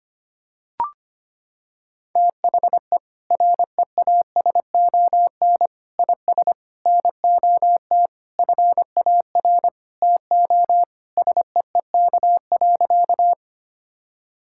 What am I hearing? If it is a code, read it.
Morse code, heard as a keyed tone, about 25 wpm, 700 Hz. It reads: THE REASON IS NOT FAR TO SEEK.
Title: Sentences from Top 1600 Words 25wpm